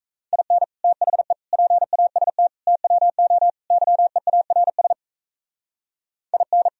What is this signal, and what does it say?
Morse code, heard as a keyed tone, about 35 wpm, 700 Hz. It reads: IN THE PAST TWO YEARS IN